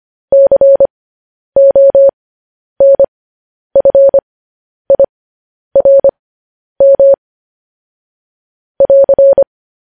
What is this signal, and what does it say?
Morse code, heard as a keyed tone, about 25 wpm, spaced as for 10 wpm, 550 Hz. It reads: CONFIRM <AR>
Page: news headlines in morse code at 10 wpm